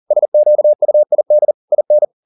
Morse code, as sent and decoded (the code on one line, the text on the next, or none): ... --.- ..- .. -.. / .. -.
SQUID IN